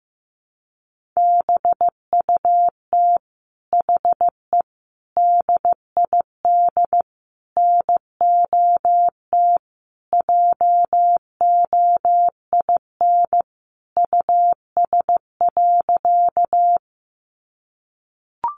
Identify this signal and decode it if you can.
Morse code, heard as a keyed tone, about 15 wpm, 700 Hz. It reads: BUT HE DID NOT JOIN US.
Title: Sentences from Top 1400 Words 15wpm